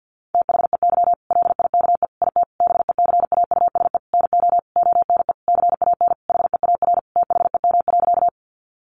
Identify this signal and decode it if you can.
Morse code, heard as a keyed tone, about 50 wpm, 700 Hz. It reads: THEY LIKE IT BECAUSE NO ONE CAN HEAR THEM.
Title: Sentences from Top 600 Words 50wpm